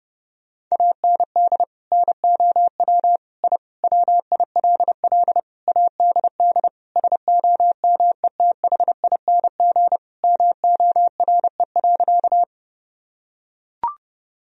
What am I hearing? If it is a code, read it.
Morse code, heard as a keyed tone, about 32 wpm, 700 Hz. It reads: AND NOW I WILL ADD SOMETHING MORE.
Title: Sentences from Top 1000 Words - Review Part 1 - 30wpm